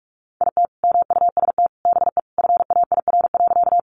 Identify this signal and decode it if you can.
Morse code, heard as a keyed tone, about 44 wpm, 700 Hz. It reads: IT MUST BE FAIR.
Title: Sentences from Top 1300 Words 45wpm